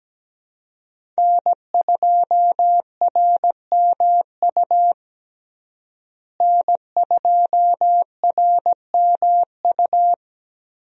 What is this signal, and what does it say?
Morse code, heard as a keyed tone, about 17 wpm, 700 Hz. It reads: N2RMU N2RMU
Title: USA Call Signs - 5 Characters - Repeated 2x - 17wpm